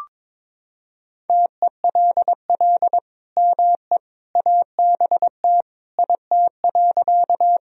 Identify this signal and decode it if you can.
Morse code, heard as a keyed tone, about 22 wpm, 700 Hz. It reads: TELL ME ABT IT.
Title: Sentences from Top 200 Words and Abbreviations 22wpm